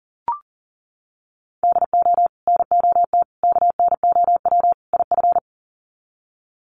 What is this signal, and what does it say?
Morse code, heard as a keyed tone, about 40 wpm, 700 Hz. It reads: DO NOT KNOW IF
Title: Sets of 4 Words 40wpm